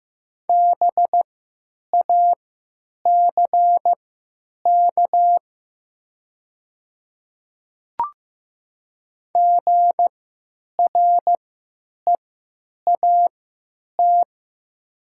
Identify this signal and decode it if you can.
Morse code, heard as a keyed tone, about 15 wpm, spaced as for 9 wpm, 700 Hz. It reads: BACK GREAT
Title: 3x Character Spacing - Top 200 Words - 15wpm